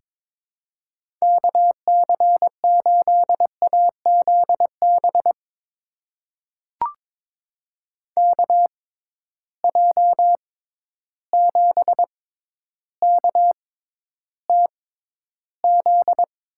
Morse code, heard as a keyed tone, about 22 wpm, spaced as for 11 wpm, 700 Hz. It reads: KC8AZB KJ7KTZ
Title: USA Call Signs - 6x Character Spacing - 22wpm